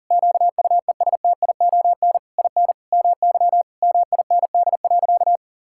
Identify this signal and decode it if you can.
Morse code, heard as a keyed tone, about 40 wpm, 700 Hz. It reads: QUESTION IN MY MIND.
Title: Sentences from Top 1000 Words - Review Part 6 - 40wpm